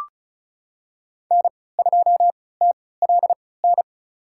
Morse code, heard as a keyed tone, about 36 wpm, spaced as for 20 wpm, 700 Hz. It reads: N2TLN
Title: USA Call Signs - 5 Characters - 3x Character Spacing - 35wpm